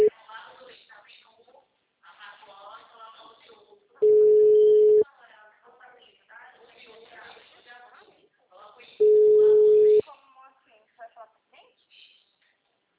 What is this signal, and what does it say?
Ringback tone (Europe, 425 Hz), with speech.
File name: out- -2015- - - .25019